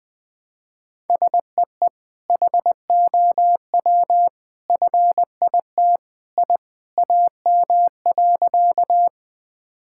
Morse code, heard as a keyed tone, about 21 wpm, 700 Hz. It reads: SEE HOW FIT I AM.
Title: Sentences from Top 1000 Words - Review Part 5 - 20wpm